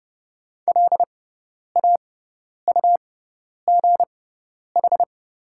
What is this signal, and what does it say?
Morse code, heard as a keyed tone, about 30 wpm, spaced as for 10 wpm, 700 Hz. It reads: LAUGH
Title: 6x Character Spacing - Top 500 Words - 30wpm